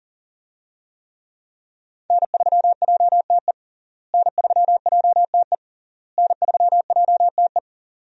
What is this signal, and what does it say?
Morse code, heard as a keyed tone, about 40 wpm, 700 Hz. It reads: N3JTE N3JTE N3JTE
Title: USA Call Signs - 5 Characters - Repeated 3x - 40wpm